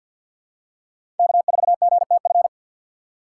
Morse code, heard as a keyed tone, about 51 wpm, 700 Hz. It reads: K4GTF
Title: USA Call Signs - 5 Characters - 6x Character Spacing - 50wpm